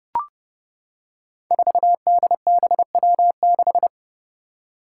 Morse code, heard as a keyed tone, about 30 wpm, 700 Hz. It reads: 4DBW6
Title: Five Challenging Letter-Number Combinations 30wpm